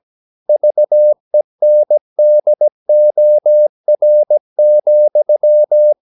Morse code, heard as a keyed tone, about 17 wpm, 600 Hz. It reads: VENDOR,
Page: Romanen Never af Ken Follet på engelsk.